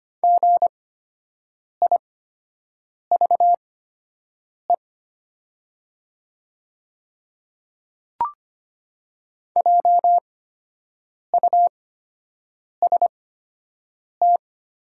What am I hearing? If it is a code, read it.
Morse code, heard as a keyed tone, about 25 wpm, spaced as for 7 wpm, 700 Hz. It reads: GIVE JUST
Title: 8x Character Spacing - Top 200 Words - 25wpm